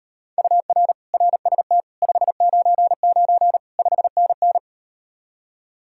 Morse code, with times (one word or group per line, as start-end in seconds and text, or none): 0.4-0.9 UR
1.1-1.8 RST
2.0-3.6 599
3.8-4.6 5NN